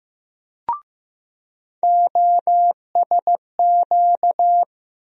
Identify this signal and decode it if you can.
Morse code, heard as a keyed tone, about 15 wpm, 700 Hz. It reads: OSQ
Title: USA Call Sign Postfixes - 3 Letters 15wpm